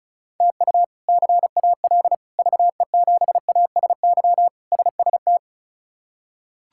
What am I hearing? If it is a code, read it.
Morse code, heard as a keyed tone, about 34 wpm, 700 Hz. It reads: TU CAL VE7ASY SST
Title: K1USN SST QSO Practice - 35wpm